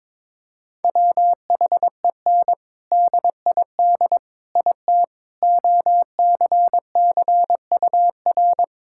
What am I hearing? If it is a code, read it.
Morse code, heard as a keyed tone, about 22 wpm, 700 Hz. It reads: WHEN DID IT OCCUR
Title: Sentences from Top 1000 Words - Review Part 6 - 22wpm